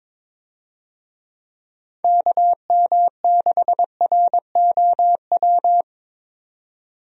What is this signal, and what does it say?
Morse code, heard as a keyed tone, about 22 wpm, 700 Hz. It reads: KM6ROW
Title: USA Call Signs - 6 Characters - 22wpm